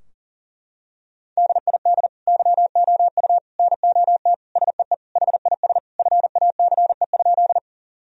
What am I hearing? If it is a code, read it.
Morse code, heard as a keyed tone, about 37 wpm, 700 Hz. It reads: DID YOU NOT SEE HIS FACE?
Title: Sentences from Top 1000 Words - Review Part 2 - 40wpm